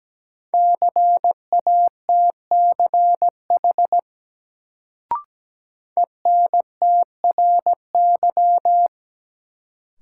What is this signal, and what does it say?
Morse code, heard as a keyed tone, about 17 wpm, 700 Hz. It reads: CATCH ENTRY
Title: Sentences from Top 1500 Words - New Word Drill Alt - 17wpm